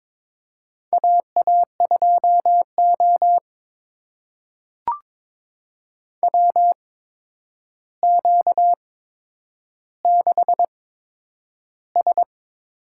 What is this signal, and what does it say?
Morse code, heard as a keyed tone, about 22 wpm, spaced as for 9 wpm, 700 Hz. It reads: AA2O WQ6S
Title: USA Call Signs - 4 Characters - 8x Character Spacing - 22wpm